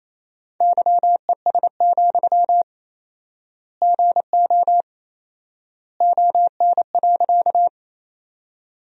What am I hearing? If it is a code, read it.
Morse code, heard as a keyed tone, about 28 wpm, 700 Hz. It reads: YES, GO ON.
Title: 4x Word-Spacing - Sentences from Top 500 Words - 28wpm